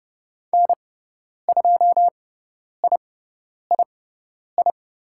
Morse code, heard as a keyed tone, about 30 wpm, spaced as for 10 wpm, 700 Hz. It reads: N2III
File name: Lesson 17g - Letter u - Call Signs 30wpm